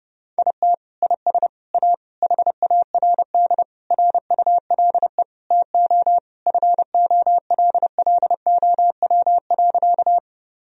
Morse code, heard as a keyed tone, about 29 wpm, 700 Hz. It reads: IT IS A HARD RULE TO FOLLOW.
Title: Sentences from Top 1000 Words - Review Part 2 - 30wpm